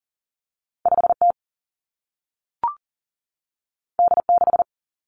Morse code, heard as a keyed tone, about 38 wpm, 700 Hz. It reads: 5T D6
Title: Two Challenging Letter-Number Combinations 40wpm